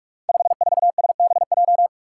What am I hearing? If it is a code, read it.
Morse code, heard as a keyed tone, about 44 wpm, 700 Hz. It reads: 54SBJ
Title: Five Challenging Letter-Number Combinations 45wpm